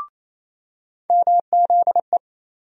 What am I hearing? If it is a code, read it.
Morse code, heard as a keyed tone, about 28 wpm, 700 Hz. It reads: MZE
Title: USA Call Sign Postfixes - 3 Letters 28wpm